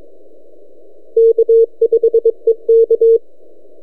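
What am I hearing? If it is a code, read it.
Morse code, heard as a keyed tone, about 24 wpm, 460 Hz. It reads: K5EK
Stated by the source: E6XG heard on 10 m CW, recorded by @n6ml.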